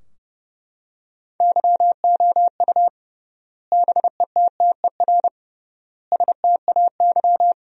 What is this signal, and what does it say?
Morse code, heard as a keyed tone, about 32 wpm, 700 Hz. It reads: YOU BETTER STAY
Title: 3x Word-Spacing - Sentences from Top 500 Words - 30wpm